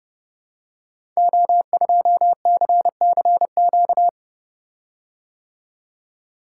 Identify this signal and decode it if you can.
Morse code, heard as a keyed tone, about 30 wpm, 700 Hz. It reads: O2CCQ
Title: Code Groups - 5 Letter:Numbers - 30wpm